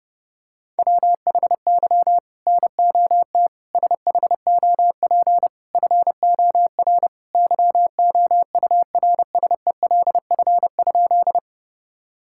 Morse code, heard as a keyed tone, about 30 wpm, 700 Hz. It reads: WHY NOT SHOP FOR YOURSELF?
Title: Sentences from Top 1600 Words 30wpm